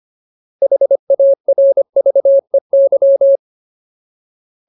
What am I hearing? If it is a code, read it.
Morse code, heard as a keyed tone, about 25 wpm, 550 Hz. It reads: HARVEY
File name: Ultimate Word List / 25WPM Ultimate